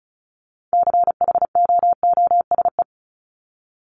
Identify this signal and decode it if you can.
Morse code, heard as a keyed tone, about 34 wpm, 700 Hz. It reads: CHOOSE
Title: Top 501-750 Words 35wpm